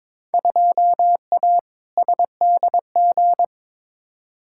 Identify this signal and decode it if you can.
Morse code, heard as a keyed tone, about 21 wpm, 700 Hz. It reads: 2A SDG
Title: Field Day Exchanges Encore - 22wpm